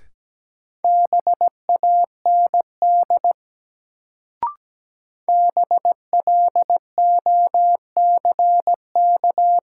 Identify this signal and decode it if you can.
Morse code, heard as a keyed tone, about 17 wpm, 700 Hz. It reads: BAND BLOCK
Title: Sentences from Top 1600 Words - New Word Drill - 17wpm